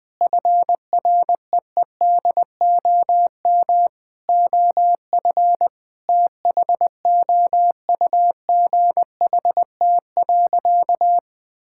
Morse code, heard as a keyed tone, about 21 wpm, 700 Hz. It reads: FREEDOM OF THOUGHT.
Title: Sentences from Top 1000 Words - Review Part 5 - 20wpm